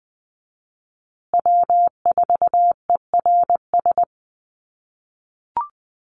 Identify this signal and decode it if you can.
Morse code, heard as a keyed tone, about 20 wpm, 700 Hz. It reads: W4ERS
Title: USA Call Signs Repeated 2x 20wpm